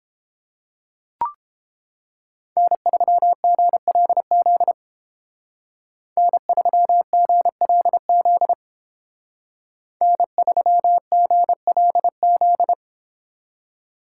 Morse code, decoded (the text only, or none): N3GLZ N3GLZ N3GLZ